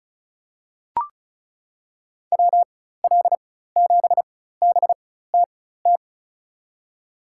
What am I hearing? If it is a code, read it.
Morse code, heard as a keyed tone, about 36 wpm, spaced as for 16 wpm, 700 Hz. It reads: WL7BTT
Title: USA Call Signs - 4x Character Spacing - 35wpm